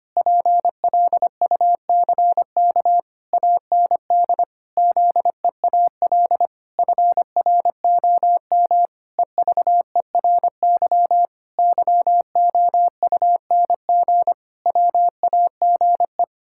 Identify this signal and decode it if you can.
Morse code, heard as a keyed tone, about 25 wpm, 700 Hz. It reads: PLUCK AND ZEAL FROM EVERY YOUNG WAGE
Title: Pangram Challenge v2 - 25wpm